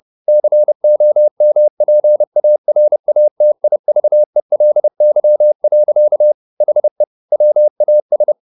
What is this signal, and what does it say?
Morse code, heard as a keyed tone, about 30 wpm, 600 Hz. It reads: COMPARATIVELY. HE WAS